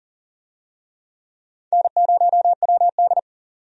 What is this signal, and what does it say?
Morse code, heard as a keyed tone, about 38 wpm, 700 Hz. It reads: N0WD
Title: USA Call Signs - 4 Characters - 40wpm